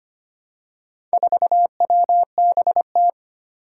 Morse code, heard as a keyed tone, about 25 wpm, 700 Hz. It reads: 4WBT
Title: Four Challenging Letter-Number Combinations 25wpm